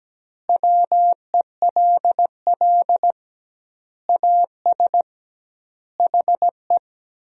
Morse code, heard as a keyed tone, about 17 wpm, 700 Hz. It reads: WELL AS HE
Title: 2x Word-Spacing - Sentences from Top 500 Words - 17wpm